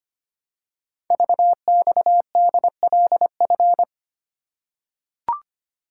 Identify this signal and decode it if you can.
Morse code, heard as a keyed tone, about 25 wpm, 700 Hz. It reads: VXDLF
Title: Code Groups - 5 Letters - 25wpm